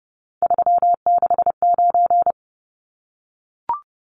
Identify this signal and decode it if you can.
Morse code, heard as a keyed tone, about 30 wpm, 700 Hz. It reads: 369
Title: Numbers - 3-Digit 30wpm